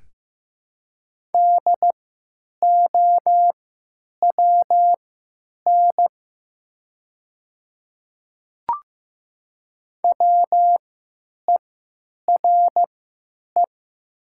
Morse code, heard as a keyed tone, about 15 wpm, spaced as for 9 wpm, 700 Hz. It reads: DOWN WERE
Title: 3x Character Spacing - Top 100 Words - 15wpm